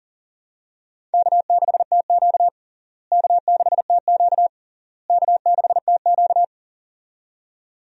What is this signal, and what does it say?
Morse code, heard as a keyed tone, about 40 wpm, 700 Hz. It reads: K6TQ K6TQ K6TQ
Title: USA Call Signs - 4 Characters - Repeated 3x - 40wpm